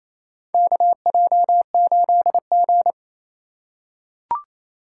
Morse code, heard as a keyed tone, about 27 wpm, 700 Hz. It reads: KJ8G
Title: USA Call Signs - 2x Character Spacing - 28wpm